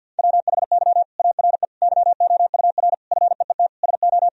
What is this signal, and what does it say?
Morse code, heard as a keyed tone, about 50 wpm, 700 Hz. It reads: WHY ARE YOUR FEET SO
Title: Sentences from Top 1000 Words - Review Part 2 - 50wpm